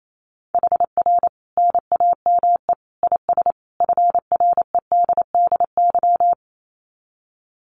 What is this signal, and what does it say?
Morse code, heard as a keyed tone, about 28 wpm, 700 Hz. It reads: HR NAME IS FREDDY
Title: Common QSO Phrases 28wpm